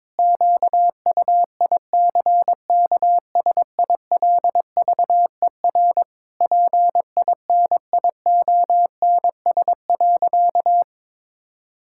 Morse code, heard as a keyed tone, about 22 wpm, 700 Hz. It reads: QUICKSILVER PINIONS.
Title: Pangram Challenge v2 - 22wpm